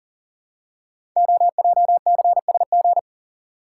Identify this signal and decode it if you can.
Morse code, heard as a keyed tone, about 43 wpm, 700 Hz. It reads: OJCSG
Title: Code Groups - 5 Letters - 40wpm